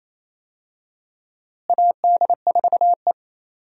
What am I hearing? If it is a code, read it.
Morse code, heard as a keyed tone, about 28 wpm, 700 Hz. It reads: AD4E
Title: USA Call Signs - 4 Characters - Repeated 2x - 28wpm